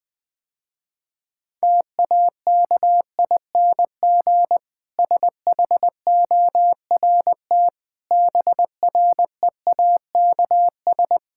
Morse code, heard as a keyed tone, about 20 wpm, 700 Hz. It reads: TAKING SHORT BREAKS
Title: Difficult Sentences with Hints - 20wpm